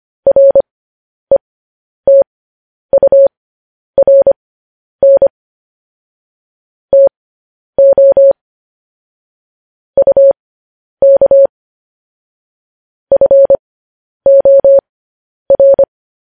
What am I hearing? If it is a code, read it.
Morse code, heard as a keyed tone, about 25 wpm, spaced as for 10 wpm, 550 Hz. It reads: RETURN TO UK FOR